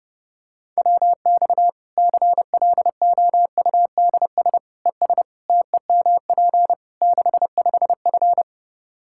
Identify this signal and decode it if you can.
Morse code, heard as a keyed tone, about 30 wpm, 700 Hz. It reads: WX CLOUDS ES TEMP 65F
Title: Practice QSO - Exchange 2 - Info - 30wpm